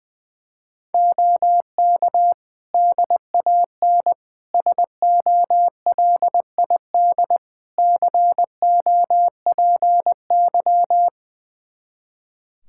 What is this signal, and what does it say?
Morse code, heard as a keyed tone, about 21 wpm, 700 Hz. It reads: OK DAN SOLID COPY
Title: Practice QSO - Exchange 3 - More Info - 20wpm